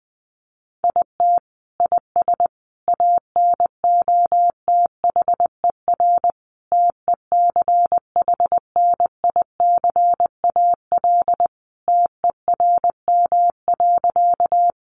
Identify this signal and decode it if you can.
Morse code, heard as a keyed tone, about 20 wpm, 700 Hz. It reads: IT IS ANOTHER TECHNICAL TERM.
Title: Sentences from Top 1100 Words 20wpm